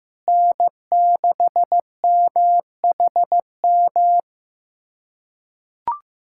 Morse code, heard as a keyed tone, about 15 wpm, 700 Hz. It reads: N6MHM
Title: USA Callsign DE Callsign - 5 Characters - 15wpm